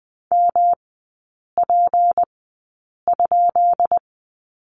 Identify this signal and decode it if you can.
Morse code, heard as a keyed tone, about 20 wpm, spaced as for 8 wpm, 700 Hz. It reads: MP?
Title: 184 Lesson 30e - Letter b - Three Characters 20wpm